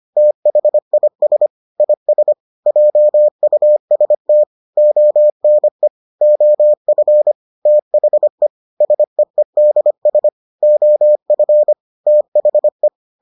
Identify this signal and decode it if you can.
Morse code, heard as a keyed tone, about 25 wpm, 600 Hz. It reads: THIS IS JUST ONE OF THE SEEDS OF THE